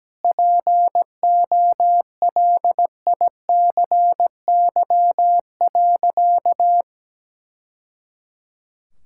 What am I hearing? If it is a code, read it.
Morse code, heard as a keyed tone, about 17 wpm, 700 Hz. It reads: POLICY.
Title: Sentences from Top 1000 Words - Review Part 5 - 17wpm